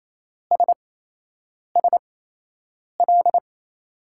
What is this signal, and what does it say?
Morse code, heard as a keyed tone, about 28 wpm, spaced as for 8 wpm, 700 Hz. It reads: SSL
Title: USA Call Signs - 5 Characters - 8x Character Spacing - 28wpm